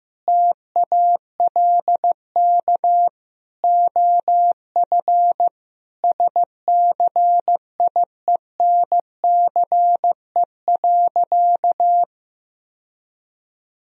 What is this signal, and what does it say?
Morse code, heard as a keyed tone, about 15 wpm, 700 Hz. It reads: TALK OF SCIENCE.
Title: Sentences from Top 1000 Words - Review Part 2 - 15wpm